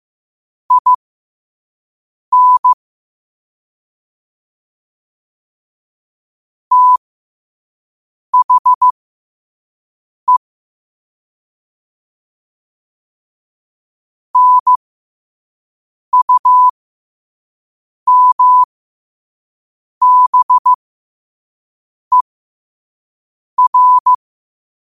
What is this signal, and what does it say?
Morse code, heard as a keyed tone, about 15 wpm, spaced as for 5 wpm, 1000 Hz. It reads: IN THE NUMBER
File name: 5wpm